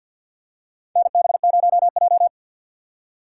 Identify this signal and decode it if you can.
Morse code, heard as a keyed tone, about 52 wpm, 700 Hz. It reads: NB0J
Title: USA Call Signs - 2x Character Spacing - 50wpm